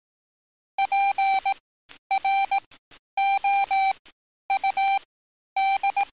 Morse code, heard as a keyed tone, about 18 wpm, spaced as for 10 wpm, 780 Hz. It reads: PROUD
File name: training / NAVPERS 11049to11060 / NAVPERS 11050 RA RB